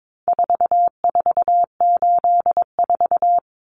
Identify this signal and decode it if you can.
Morse code, heard as a keyed tone, about 22 wpm, 700 Hz. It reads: 4484
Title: Common Cut Numbers - 4-Digit 22wpm